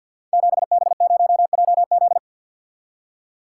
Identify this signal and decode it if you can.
Morse code, heard as a keyed tone, about 46 wpm, 700 Hz. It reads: 7B0JZ
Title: Code Groups - 5 Letter:Numbers - 50wpm